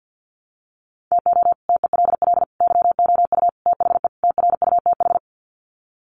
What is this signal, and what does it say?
Morse code, heard as a keyed tone, about 50 wpm, 700 Hz. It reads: TO TELL YOU THE TRUTH
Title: Sets of 5 Words 50wpm